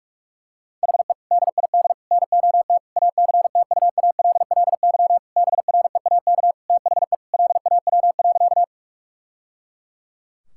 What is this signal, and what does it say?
Morse code, heard as a keyed tone, about 46 wpm, 700 Hz. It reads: HE DID NOT ACTUALLY BREAK THE LAW.
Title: Sentences from Top 1000 Words - Review Part 6 - 45wpm